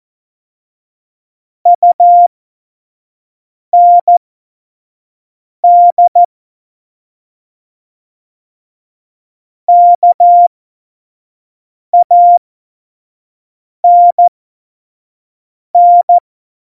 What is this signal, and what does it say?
Morse code, heard as a keyed tone, about 14 wpm, spaced as for 5 wpm, 700 Hz. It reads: UND KANN
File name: Morsekurs / Witze / Witz 19 Mai 2020, 25 F70